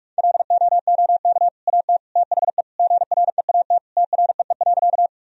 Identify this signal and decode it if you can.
Morse code, heard as a keyed tone, about 46 wpm, 700 Hz. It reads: LOOK AT THE GREAT TREE.
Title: Sentences from Top 1000 Words - Review Part 1 - 45wpm